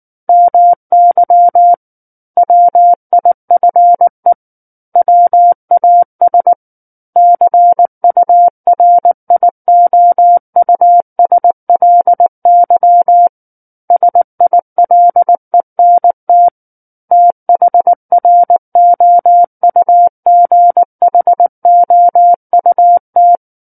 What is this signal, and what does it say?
Morse code, heard as a keyed tone, about 19 wpm, 700 Hz. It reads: MY WIFE WAS CURIOUSLY SILENT THROUGHOUT
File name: Book One / 10-Chapter 10 - 19 WPM